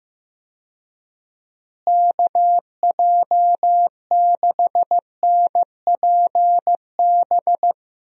Morse code, heard as a keyed tone, about 15 wpm, 700 Hz. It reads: KJ6NPB
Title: USA Call Signs - 6 Characters - Repeated 3x - 15wpm